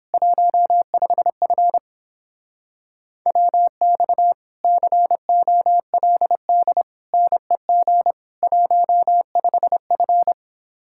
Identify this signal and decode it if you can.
Morse code, heard as a keyed tone, about 29 wpm, 700 Hz. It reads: 15F WX COLD NEG 15F